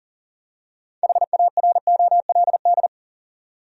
Morse code, heard as a keyed tone, about 38 wpm, 700 Hz. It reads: HAROLD
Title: USA Common Names - 40wpm